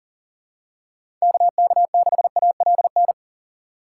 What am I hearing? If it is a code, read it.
Morse code, heard as a keyed tone, about 40 wpm, 700 Hz. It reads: KK6ALN
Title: USA Call Signs - 6 Characters - 4x Character Spacing - 40wpm